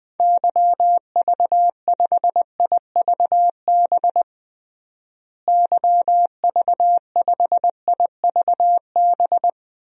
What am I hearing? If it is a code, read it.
Morse code, heard as a keyed tone, about 20 wpm, 700 Hz. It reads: YV5IVB YV5IVB
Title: DXCC-100 Call Signs - 6 Characters - Repeated 3x - 20wpm